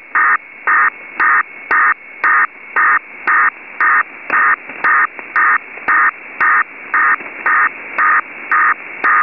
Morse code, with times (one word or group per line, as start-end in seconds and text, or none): none